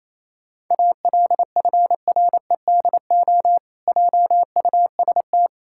Morse code, heard as a keyed tone, about 28 wpm, 700 Hz. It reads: ALFREDO JUST
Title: Pangram Challenge v2 - 28wpm